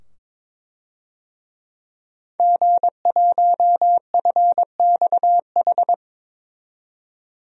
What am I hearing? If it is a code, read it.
Morse code, heard as a keyed tone, about 22 wpm, 700 Hz. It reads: G1FXH
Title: DXCC-100 Call Signs - 5 Characters - 22wpm